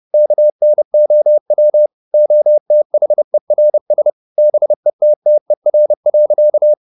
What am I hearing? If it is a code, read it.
Morse code, heard as a keyed tone, about 30 wpm, 600 Hz. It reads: KNOW OTHERS BETTER.
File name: the old man and the sea / the old man and the sea 1 30wpm